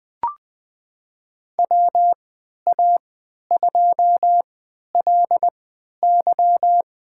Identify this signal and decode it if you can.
Morse code, heard as a keyed tone, about 20 wpm, spaced as for 11 wpm, 700 Hz. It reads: WA2LY
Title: USA Call Signs - 3x Character Spacing - 20wpm